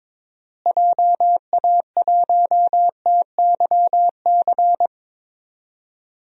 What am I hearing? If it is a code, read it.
Morse code, heard as a keyed tone, about 22 wpm, 700 Hz. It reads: JA1TYC
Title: DXCC-100 Call Signs - 6 Characters - Repeated 2x - 22wpm